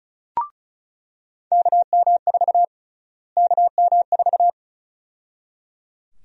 Morse code, heard as a keyed tone, about 34 wpm, 700 Hz. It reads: KM4 KM4
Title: USA Call Sign Prefixes - Repeated 2x - 35wpm